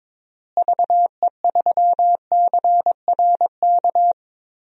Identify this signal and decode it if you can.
Morse code, heard as a keyed tone, about 22 wpm, 700 Hz. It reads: VE3CRK
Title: DXCC-100 Call Signs - 6 Characters - Repeated 3x - 22wpm